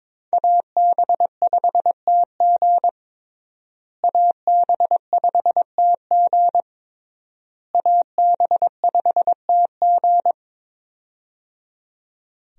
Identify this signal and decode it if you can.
Morse code, heard as a keyed tone, about 22 wpm, 700 Hz. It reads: AB5TG AB5TG AB5TG
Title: USA Call Signs - 5 Characters - Repeated 3x - 22wpm